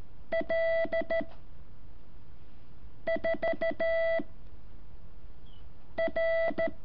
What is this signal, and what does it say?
Morse code, heard as a keyed tone, about 13 wpm, 660 Hz. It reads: L 4 R
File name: curso / ejercicio 7